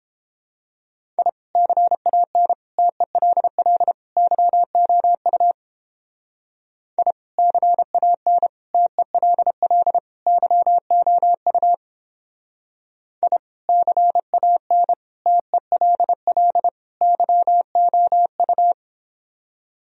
Morse code, heard as a keyed tone, about 29 wpm, 700 Hz. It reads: I CAN TELL YOU I CAN TELL YOU I CAN TELL YOU